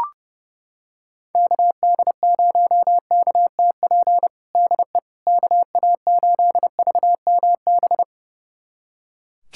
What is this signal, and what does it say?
Morse code, heard as a keyed tone, about 30 wpm, 700 Hz. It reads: KD0KTP DE KA8VMB
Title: USA Callsign DE Callsign 30wpm